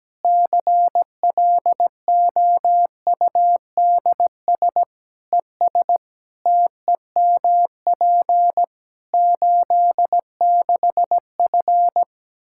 Morse code, heard as a keyed tone, about 17 wpm, 700 Hz. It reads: CLOUDS ES TEMP 86F
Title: Practice QSO - Exchange 2 - Info - 17wpm